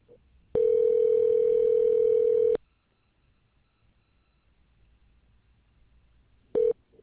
Ringback tone (North America), 440 + 480 Hz.